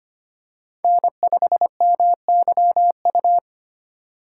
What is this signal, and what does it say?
Morse code, heard as a keyed tone, about 25 wpm, 700 Hz. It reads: N5MYU
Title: USA Call Signs - 5 Characters - 4x Character Spacing - 25wpm